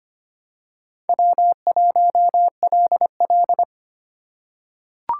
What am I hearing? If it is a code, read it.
Morse code, heard as a keyed tone, about 25 wpm, 700 Hz. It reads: W1LL
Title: USA Call Signs - 4 Characters - 3x Character Spacing - 25wpm